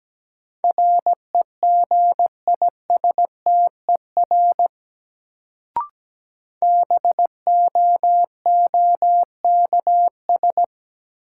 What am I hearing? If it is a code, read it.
Morse code, heard as a keyed tone, about 17 wpm, 700 Hz. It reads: REGISTER BOOKS
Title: Sentences from Top 1200 Words - New Word Drill Alt - 17wpm